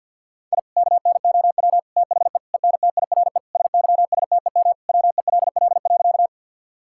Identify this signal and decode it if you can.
Morse code, heard as a keyed tone, about 50 wpm, 700 Hz. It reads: I KNOW THE ENTIRE SYSTEM WELL.
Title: Sentences from Top 1000 Words - Review Part 5 - 50wpm